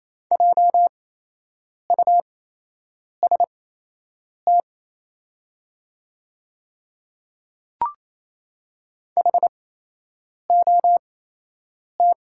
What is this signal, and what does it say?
Morse code, heard as a keyed tone, about 27 wpm, spaced as for 8 wpm, 690 Hz. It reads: JUST HOT
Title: 8x Character Spacing - Top 300 Words - 28wpm